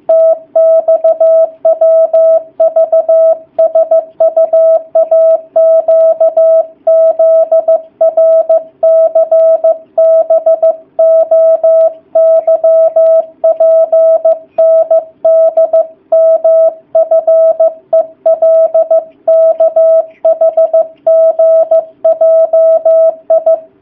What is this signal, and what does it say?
Morse code, heard as a keyed tone, about 15 wpm, 660 Hz. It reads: TXWVSUAQZRCBOYPNDMFELKHGJI
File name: jh8cbh / 2019cw / sp15- w